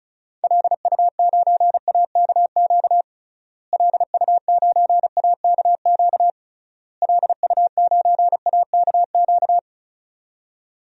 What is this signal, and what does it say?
Morse code, heard as a keyed tone, about 36 wpm, 700 Hz. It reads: LU9AKQ LU9AKQ LU9AKQ
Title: DXCC-100 Call Signs - 6 Characters - Repeated 3x - 35wpm